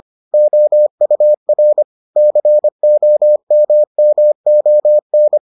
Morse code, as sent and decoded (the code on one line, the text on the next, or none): --- ..- .-. / -.-. --- -- -- --- -.
OUR COMMON